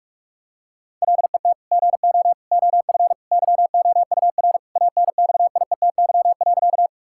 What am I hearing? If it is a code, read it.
Morse code, heard as a keyed tone, about 41 wpm, 700 Hz. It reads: LET GO OF YOUR ANXIETY.
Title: Sentences from Top 1600 Words 45wpm